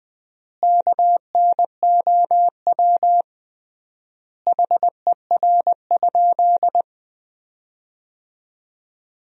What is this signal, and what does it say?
Morse code, heard as a keyed tone, about 20 wpm, 700 Hz. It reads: KNOW HER?
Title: 3x Word-Spacing - Sentences from Top 100 Words - 20wpm